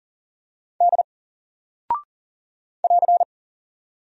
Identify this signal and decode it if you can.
Morse code, heard as a keyed tone, about 40 wpm, 700 Hz. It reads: D <AR>
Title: Single Character (Letter:Number:Punctuation:Prosign) Alt 40wpm